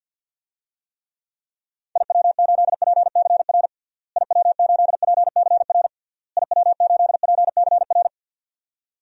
Morse code, heard as a keyed tone, about 51 wpm, 700 Hz. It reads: IW8PCR IW8PCR IW8PCR
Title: DXCC-100 Call Signs - 6 Characters - Repeated 3x - 50wpm